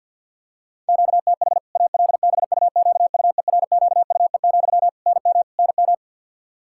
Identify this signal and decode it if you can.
Morse code, heard as a keyed tone, about 50 wpm, 700 Hz. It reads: QTH ALBUQUERQUE, NM NM
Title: K1USN SST QSO Practice 50wpm